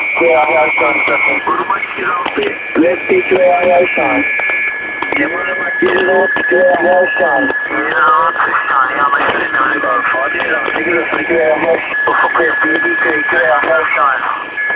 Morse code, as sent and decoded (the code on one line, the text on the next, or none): none